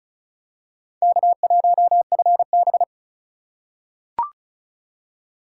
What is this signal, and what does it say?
Morse code, heard as a keyed tone, about 34 wpm, 700 Hz. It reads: K1FB
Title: USA Call Signs - 4 Characters - 3x Character Spacing - 35wpm